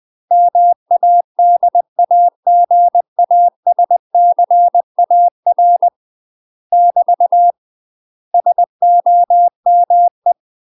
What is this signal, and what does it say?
Morse code, heard as a keyed tone, about 20 wpm, 700 Hz. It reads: MADAGASCAR <BT> SOME
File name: UK MASTER QSOs TRAINING / 20WPM / UK MASTER QSOs 0012